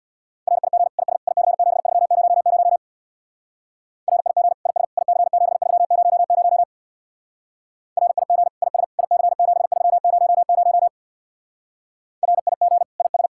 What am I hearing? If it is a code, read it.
Morse code, heard as a keyed tone, about 63 wpm, 700 Hz. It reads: RIG IS IC7300 RIG IS IC7300 RIG IS IC7300 RIG IS